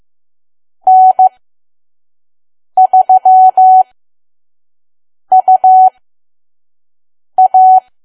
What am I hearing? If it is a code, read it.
Morse code, heard as a keyed tone, about 15 wpm, spaced as for 5 wpm, 750 Hz. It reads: N3UA
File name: Archive / 5 WPM files / PM